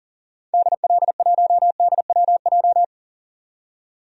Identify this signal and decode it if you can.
Morse code, heard as a keyed tone, about 38 wpm, 700 Hz. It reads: DL1DWJ
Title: DXCC-100 Call Signs - 6 Characters - 40wpm